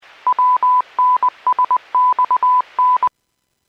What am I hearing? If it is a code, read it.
Morse code, heard as a keyed tone, about 20 wpm, 1000 Hz. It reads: WNSXN
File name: Businesses / 00406 02-27 Businesses 451.25 WPS